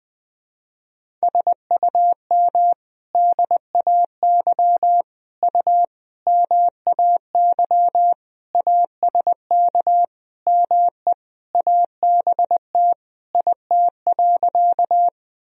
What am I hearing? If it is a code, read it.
Morse code, heard as a keyed tone, about 21 wpm, 700 Hz. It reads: SUM DAY U MAY ASK ME ABT IT.
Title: Sentences from Top 200 Words and Abbreviations 20wpm